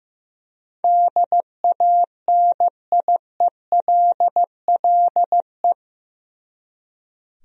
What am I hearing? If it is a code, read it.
Morse code, heard as a keyed tone, about 15 wpm, 700 Hz. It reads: DANIELLE
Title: USA Common Names - 15wpm